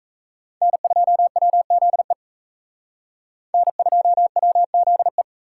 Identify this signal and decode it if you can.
Morse code, heard as a keyed tone, about 34 wpm, 700 Hz. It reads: N2WZE N2WZE
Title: Speed Racing - USA 1x3 Call Signs 28wpm